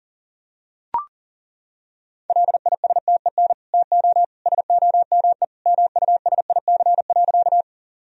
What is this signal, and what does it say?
Morse code, heard as a keyed tone, about 37 wpm, 700 Hz. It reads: LISTEN TO SOME MUSIC.
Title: Sentences from Top 1000 Words - Review Part 1 - 40wpm